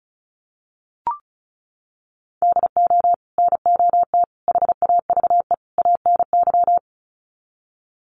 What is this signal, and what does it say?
Morse code, heard as a keyed tone, about 34 wpm, 700 Hz. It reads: DO NOT HAVE ANY
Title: Sets of 4 Words 35wpm